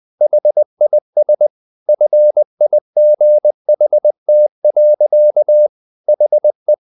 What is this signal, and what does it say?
Morse code, heard as a keyed tone, about 20 wpm, 600 Hz. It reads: HIS FIGHT. HE
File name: the old man and the sea / the old man and the sea part 1 20wpm